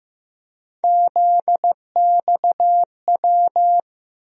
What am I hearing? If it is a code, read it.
Morse code, heard as a keyed tone, about 15 wpm, 700 Hz. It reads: ZXW
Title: USA Call Sign Postfixes - 3 Letters 15wpm